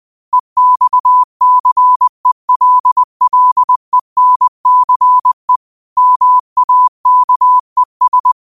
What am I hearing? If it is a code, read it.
Morse code, heard as a keyed tone, about 20 wpm, 1000 Hz. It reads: EXCELLENCE MAKES
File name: qotdfiles / 20wpm